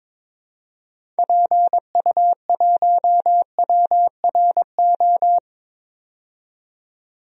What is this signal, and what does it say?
Morse code, heard as a keyed tone, about 22 wpm, 700 Hz. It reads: PU1WRO
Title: DXCC-100 Call Signs - 6 Characters - 22wpm